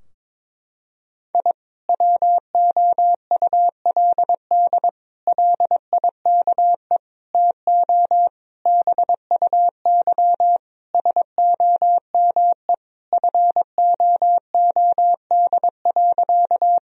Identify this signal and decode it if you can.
Morse code, heard as a keyed tone, about 22 wpm, 700 Hz. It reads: I WOULD LIKE TO BUY SOME FOOD.
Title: Sentences from Top 1000 Words - Review Part 4 - 22wpm